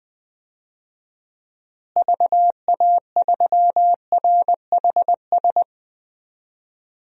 Morse code, heard as a keyed tone, about 20 wpm, 700 Hz. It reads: VA3RHS
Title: DXCC-100 Call Signs - 6 Characters - Repeated 2x - 20wpm